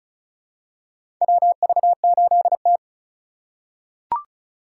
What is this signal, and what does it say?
Morse code, heard as a keyed tone, about 34 wpm, 700 Hz. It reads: WV8T
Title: USA Call Signs - 4 Characters - 4x Character Spacing - 35wpm